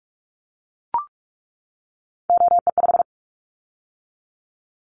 Morse code, heard as a keyed tone, about 44 wpm, 700 Hz. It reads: OE5
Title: Callsign Prefixes for Top 100 DXCC - 45wpm